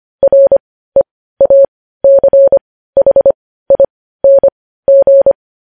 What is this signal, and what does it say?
Morse code, heard as a keyed tone, about 25 wpm, spaced as for 15 wpm, 550 Hz. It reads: REACHING
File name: podcast / News headlines-1x-25x15